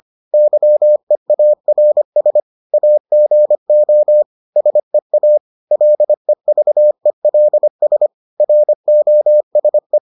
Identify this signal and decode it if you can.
Morse code, heard as a keyed tone, about 25 wpm, 600 Hz. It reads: YEARS AGO SEA LEVELS ROSE